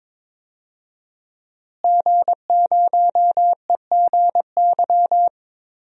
Morse code, heard as a keyed tone, about 22 wpm, 700 Hz. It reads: G0EGY
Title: DXCC-100 Call Signs - 5 Characters - Repeated 3x - 22wpm